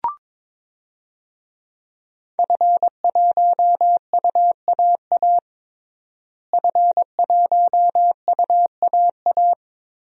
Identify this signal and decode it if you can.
Morse code, heard as a keyed tone, about 22 wpm, 700 Hz. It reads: F1UAA F1UAA
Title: DXCC-100 Call Signs - 5 Characters - Repeated 3x - 22wpm